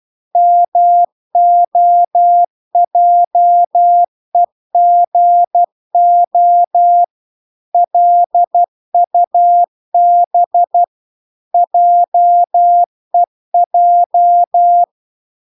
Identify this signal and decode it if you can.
Morse code, heard as a keyed tone, about 12 wpm, 700 Hz. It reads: MOJEGO LUB JEJ